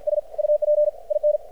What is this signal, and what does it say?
Morse code, heard as a keyed tone, about 50 wpm, 600 Hz. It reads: S UP IN
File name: archived / CRSnet